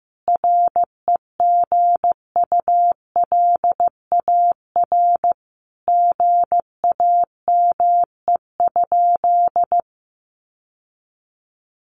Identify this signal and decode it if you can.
Morse code, heard as a keyed tone, about 15 wpm, 700 Hz. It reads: REGULAR GAME?
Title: Sentences from Top 900 Words 15wpm